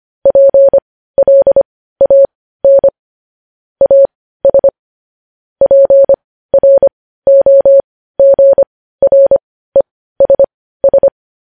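Morse code, heard as a keyed tone, about 25 wpm, spaced as for 15 wpm, 550 Hz. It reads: PLAN AS PROGRESS
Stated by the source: Morse practice of news headlines at 15 wpm.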